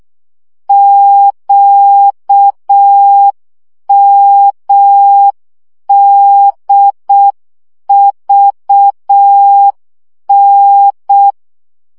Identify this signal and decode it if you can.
Morse code, heard as a keyed tone, about 6 wpm, 800 Hz. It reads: QMDVN
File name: morze vizsga 2011-2 5